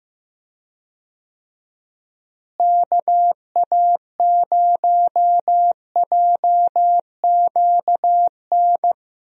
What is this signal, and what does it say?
Morse code, heard as a keyed tone, about 15 wpm, 700 Hz. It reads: KA0JQN
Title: USA Call Signs - 6 Characters - Repeated 3x - 15wpm